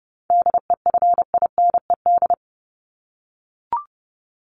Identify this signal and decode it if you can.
Morse code, heard as a keyed tone, about 30 wpm, 700 Hz. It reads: DEFINED
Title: ED Suffix - Top 100 Words with Base Words 30wpm